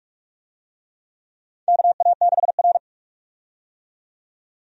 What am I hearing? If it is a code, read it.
Morse code, heard as a keyed tone, about 45 wpm, 700 Hz. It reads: KA6R
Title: USA Call Signs - 4 Characters - 45wpm